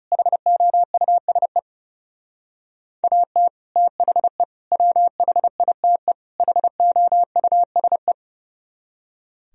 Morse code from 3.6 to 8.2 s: THE WHITE HOUSE

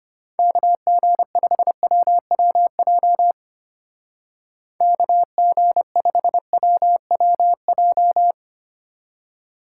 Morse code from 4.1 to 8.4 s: KG5WWJ

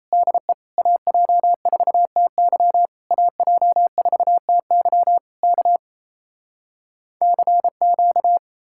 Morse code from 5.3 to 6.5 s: K

